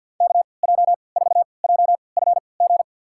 Morse code, heard as a keyed tone, about 52 wpm, spaced as for 28 wpm, 700 Hz. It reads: KJ4JFG
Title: USA Call Signs - 6 Characters - 3x Character Spacing - 50wpm